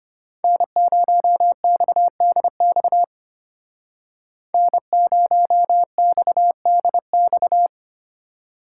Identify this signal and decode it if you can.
Morse code, heard as a keyed tone, about 33 wpm, 700 Hz. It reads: N0XDX N0XDX